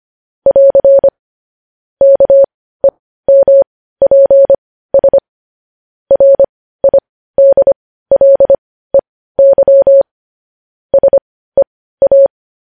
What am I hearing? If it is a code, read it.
Morse code, heard as a keyed tone, about 25 wpm, spaced as for 15 wpm, 550 Hz. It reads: <AR> KEMPS RIDLEY SEA
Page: news-headlines-in-morse-code-at-15-wpm